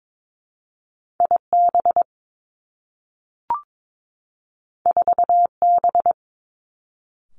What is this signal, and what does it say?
Morse code, heard as a keyed tone, about 22 wpm, 700 Hz. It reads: I B 4 B
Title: Two Challenging Letter-Number Combinations 22wpm